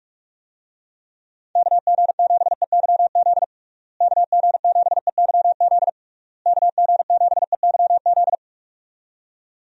Morse code, heard as a keyed tone, about 46 wpm, 700 Hz. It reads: KG7EYZ KG7EYZ KG7EYZ
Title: USA Call Signs - 6 Characters - Repeated 3x - 45wpm